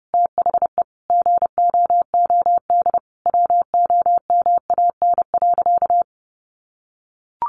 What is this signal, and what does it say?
Morse code, heard as a keyed tone, about 30 wpm, 700 Hz. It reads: THE GOOD WOMAN.
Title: Sentences from Top 1100 Words 30wpm